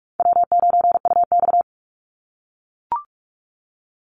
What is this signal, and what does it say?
Morse code, heard as a keyed tone, about 44 wpm, 700 Hz. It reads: W9UX
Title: NA SOTA Chaser Callsigns 45wpm